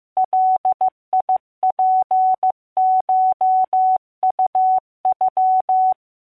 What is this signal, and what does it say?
Morse code (Japanese Wabun code), heard as a keyed tone, about 15 wpm, 750 Hz. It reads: ガツコウノ